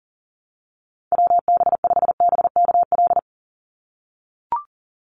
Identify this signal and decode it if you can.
Morse code, heard as a keyed tone, about 40 wpm, 700 Hz. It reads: WB5BKL
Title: NA SOTA Chaser Callsigns 40wpm